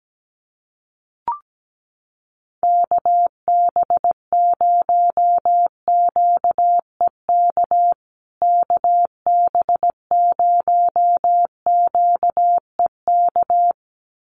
Morse code, heard as a keyed tone, about 17 wpm, 700 Hz. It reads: KB0QEK KB0QEK
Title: USA Call Signs Repeated 2x 17wpm